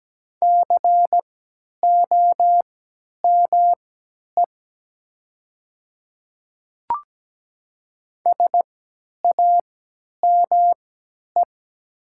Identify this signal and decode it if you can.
Morse code, heard as a keyed tone, about 17 wpm, spaced as for 10 wpm, 700 Hz. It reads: COME SAME
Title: 3x Character Spacing - Top 200 Words - 17wpm